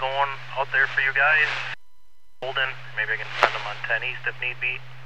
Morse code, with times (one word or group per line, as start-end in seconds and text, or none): none